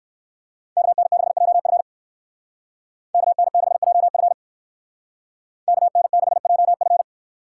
Morse code, heard as a keyed tone, about 62 wpm, 700 Hz. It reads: KN6JF KN6JF KN6JF